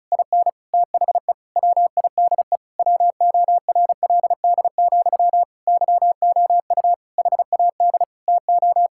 Morse code, heard as a keyed tone, about 36 wpm, 700 Hz. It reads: IN THE WIDE WORLD, YOU HAD TO
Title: Sentences from Top 1000 Words - Review Part 6 - 35wpm